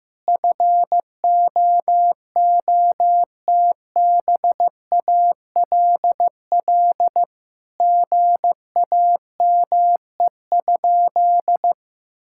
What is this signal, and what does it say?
Morse code, heard as a keyed tone, about 15 wpm, 700 Hz. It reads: FOOTBALL GAME?
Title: Sentences from Top 1600 Words 15wpm